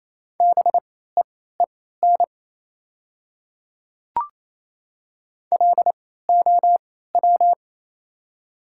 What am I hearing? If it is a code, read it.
Morse code, heard as a keyed tone, about 27 wpm, spaced as for 16 wpm, 700 Hz. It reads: BEEN LOW
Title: 3x Character Spacing - Top 200 Words - 28wpm